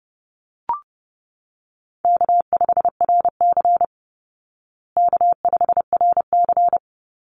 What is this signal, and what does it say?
Morse code, heard as a keyed tone, about 30 wpm, 700 Hz. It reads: K5RC K5RC
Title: Easy Call Signs Repeated 2x - 30wpm